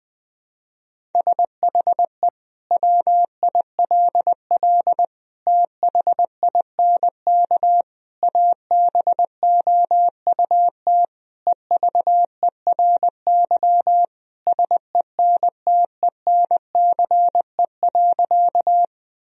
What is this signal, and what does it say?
Morse code, heard as a keyed tone, about 20 wpm, 700 Hz. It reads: SHE WILL THINK ABOUT EVERY SENTENCE.
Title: Sentences from Top 1000 Words - Review Part 1 - 20wpm